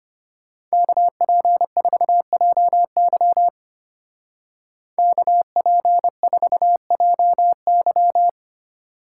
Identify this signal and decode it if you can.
Morse code, heard as a keyed tone, about 29 wpm, 700 Hz. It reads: KP4JY KP4JY
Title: Speed Racing - USA Call Signs 25wpm